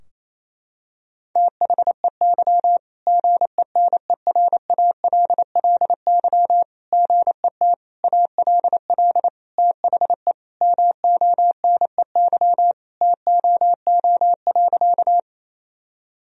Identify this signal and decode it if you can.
Morse code, heard as a keyed tone, about 27 wpm, 700 Hz. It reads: THEY GENERALLY GET ALL THE MONEY TOO.
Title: Sentences from Top 1000 Words - Review Part 4 - 28wpm